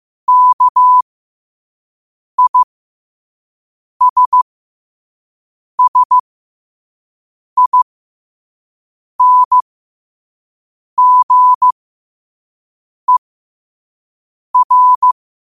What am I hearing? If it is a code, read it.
Morse code, heard as a keyed tone, about 15 wpm, spaced as for 5 wpm, 1000 Hz. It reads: KISSINGER
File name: qotdfiles / 5wpm